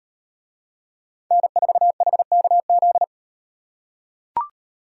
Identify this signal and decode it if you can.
Morse code, heard as a keyed tone, about 38 wpm, 700 Hz. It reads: N4HKZ